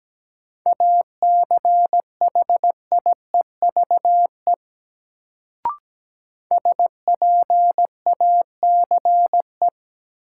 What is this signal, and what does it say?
Morse code, heard as a keyed tone, about 17 wpm, 700 Hz. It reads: ACHIEVE SPACE
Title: Sentences from Top 1000 Words - Review Part 5 - New Word Drill - 17wpm